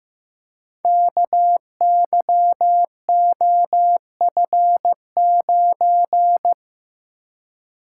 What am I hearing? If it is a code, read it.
Morse code, heard as a keyed tone, about 15 wpm, 700 Hz. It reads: KYOF9
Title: Code Groups - 5 Letter:Numbers - 15wpm